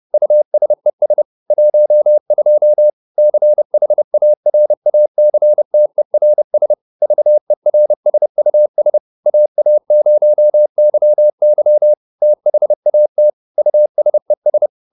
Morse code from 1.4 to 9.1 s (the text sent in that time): 12 CHARACTERS VERSUS